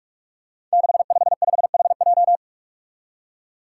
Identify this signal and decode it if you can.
Morse code, heard as a keyed tone, about 44 wpm, 700 Hz. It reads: 655HJ
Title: Five Challenging Letter-Number Combinations 45wpm